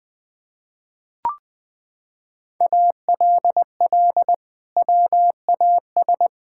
Morse code, heard as a keyed tone, about 21 wpm, 700 Hz. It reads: ALL WAS
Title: Sentences from Top 1000 Words - Review Part 6 - 20wpm